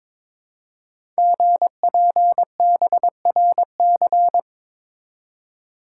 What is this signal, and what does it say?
Morse code, heard as a keyed tone, about 22 wpm, 700 Hz. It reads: GPBRC
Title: Code Groups - 5 Letters - 22wpm